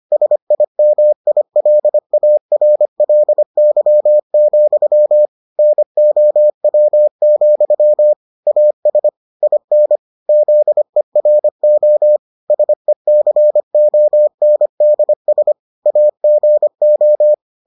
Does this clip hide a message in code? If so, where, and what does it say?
Yes: Morse code from 0.1 to 17.3 s, heard as a keyed tone, about 25 wpm, 600 Hz. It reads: SIMILARLY, NOW, AS IN ZERO SECONDS AGO